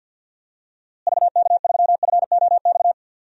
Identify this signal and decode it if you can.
Morse code, heard as a keyed tone, about 48 wpm, 700 Hz. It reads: VK3FOX
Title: DXCC-100 Call Signs - 6 Characters - Repeated 3x - 50wpm